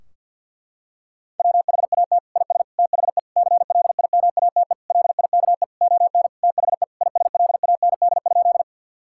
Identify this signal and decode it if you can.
Morse code, heard as a keyed tone, about 50 wpm, 700 Hz. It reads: WHAT IS THE CLIMATE LIKE ON THE ISLAND?
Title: Sentences from Top 1600 Words 50wpm